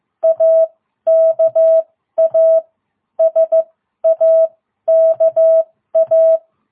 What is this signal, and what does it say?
Morse code, heard as a keyed tone, about 14 wpm, spaced as for 12 wpm, 660 Hz. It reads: AKA SAKA
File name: jh8cbh / 2022cw